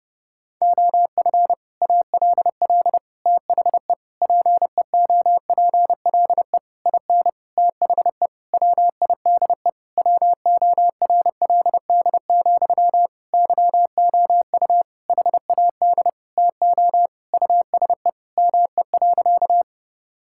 Morse code, heard as a keyed tone, about 28 wpm, 700 Hz. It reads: OF ALL THE PEOPLE IN THE WIDE WORLD, YOU HAD TO USE ME.
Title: Sentences from Top 1000 Words - Review Part 6 - 30wpm